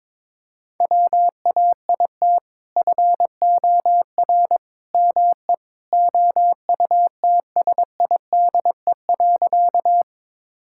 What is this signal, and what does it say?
Morse code, heard as a keyed tone, about 22 wpm, 700 Hz. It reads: WAIT FOR ME OUTSIDE.
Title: Sentences from Top 1000 Words - Review Part 6 - 22wpm